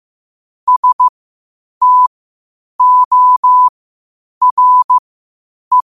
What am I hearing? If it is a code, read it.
Morse code, heard as a keyed tone, about 15 wpm, spaced as for 8 wpm, 1000 Hz. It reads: STORE
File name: qotdfiles / 8wpm